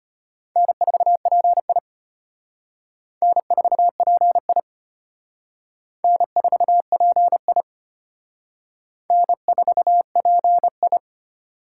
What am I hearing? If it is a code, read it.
Morse code, heard as a keyed tone, about 32 wpm, 700 Hz. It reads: N4PI N4PI N4PI N4PI